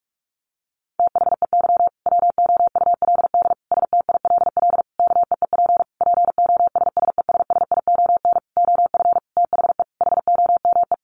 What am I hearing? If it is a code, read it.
Morse code, heard as a keyed tone, about 44 wpm, 700 Hz. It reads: THEY WOULD STILL KEEP POSSESSION OF THE HOME
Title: Sentences from Top 1400 Words 45wpm